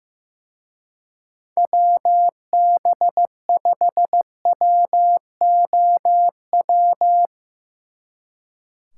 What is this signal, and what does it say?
Morse code, heard as a keyed tone, about 16 wpm, 700 Hz. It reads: WB5WOW
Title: Speed Racing - USA 2x3 Call Signs 15wpm